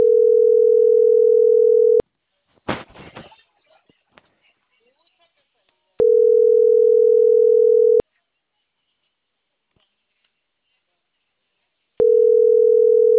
Ringback tone (North America), 440 + 480 Hz.